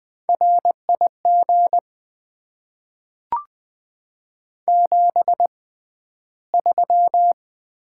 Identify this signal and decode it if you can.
Morse code, heard as a keyed tone, about 20 wpm, spaced as for 12 wpm, 690 Hz. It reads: RIG 73
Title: Top 20 QSO Elements - 6x Character Spacing - 20wpm